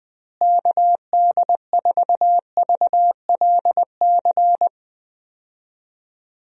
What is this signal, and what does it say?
Morse code, heard as a keyed tone, about 20 wpm, 700 Hz. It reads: KD4VLC
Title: USA Call Signs - 6 Characters - Repeated 2x - 20wpm